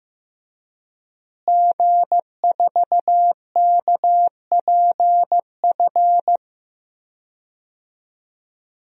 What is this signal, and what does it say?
Morse code, heard as a keyed tone, about 15 wpm, 700 Hz. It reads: G4KPF
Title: DXCC-100 Call Signs - 5 Characters - Repeated 3x - 15wpm